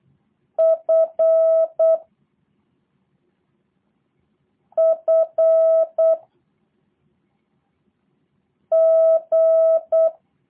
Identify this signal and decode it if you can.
Morse code, heard as a keyed tone, about 8 wpm, 650 Hz. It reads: F F G
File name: jh8cbh / 2023cw / review1-4